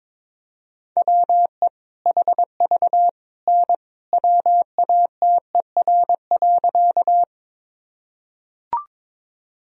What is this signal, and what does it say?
Morse code, heard as a keyed tone, about 22 wpm, 700 Hz. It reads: WE HV N WATER.
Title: Sentences from Top 100 Words and Abbreviations 22wpm